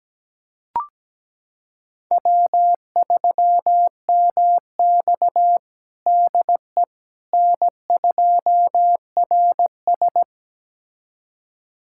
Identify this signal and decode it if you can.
Morse code, heard as a keyed tone, about 17 wpm, 700 Hz. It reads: W3MX DE N2RS
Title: USA Callsign DE Callsign - 4 Characters - 17wpm